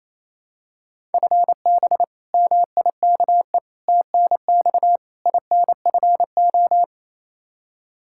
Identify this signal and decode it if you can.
Morse code, heard as a keyed tone, about 28 wpm, 700 Hz. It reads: FB MIKE TNX INFO
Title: Practice QSO - Exchange 3 - More Info - 28wpm